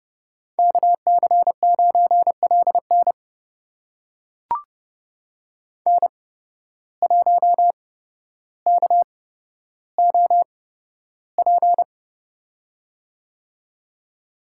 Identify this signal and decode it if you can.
Morse code, heard as a keyed tone, about 30 wpm, spaced as for 13 wpm, 700 Hz. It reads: KC9LN N1KOP
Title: USA Call Signs - 8x Character Spacing - 30wpm